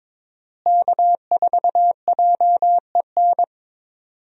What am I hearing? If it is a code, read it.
Morse code, heard as a keyed tone, about 23 wpm, 700 Hz. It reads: K4JEN
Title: USA Call Signs - 5 Characters - Repeated 3x - 22wpm